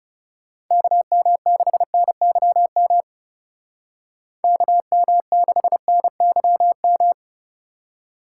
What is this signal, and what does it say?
Morse code, heard as a keyed tone, about 38 wpm, 700 Hz. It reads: KM6NYM KM6NYM
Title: Speed Racing - USA 2x3 Call Signs 30wpm